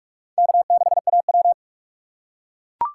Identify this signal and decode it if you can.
Morse code, heard as a keyed tone, about 41 wpm, 690 Hz. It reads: K6AW
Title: USA Call Signs - 4x Character Spacing - 45wpm